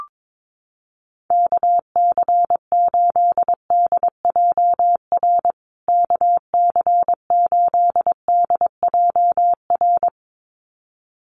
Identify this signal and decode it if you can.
Morse code, heard as a keyed tone, about 22 wpm, 700 Hz. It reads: KC8DJR KC8DJR
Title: USA Call Signs Repeated 2x 22wpm